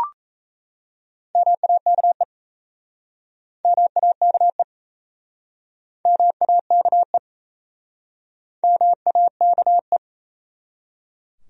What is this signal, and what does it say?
Morse code, heard as a keyed tone, about 37 wpm, 700 Hz. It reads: MAKE MAKE MAKE MAKE